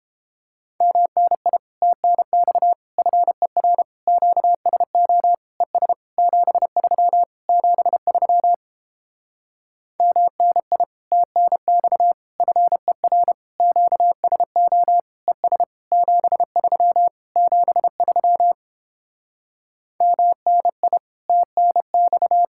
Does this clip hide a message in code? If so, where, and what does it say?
Yes: Morse code from 0.8 to 22.4 s, heard as a keyed tone, about 29 wpm, 700 Hz. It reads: MNI TNX FER QSO ES 73 73 MNI TNX FER QSO ES 73 73 MNI TNX